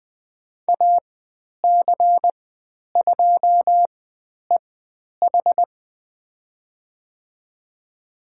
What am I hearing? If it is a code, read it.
Morse code, heard as a keyed tone, about 20 wpm, spaced as for 10 wpm, 700 Hz. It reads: AC2EH
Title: 095 Lesson 16g - Letter c - Call Signs 20wpm